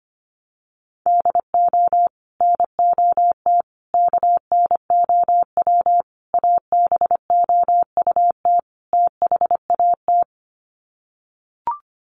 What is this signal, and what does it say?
Morse code, heard as a keyed tone, about 25 wpm, 700 Hz. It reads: DO NOT KNOW ABOUT THAT
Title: Sets of 5 Words Encore2 25wpm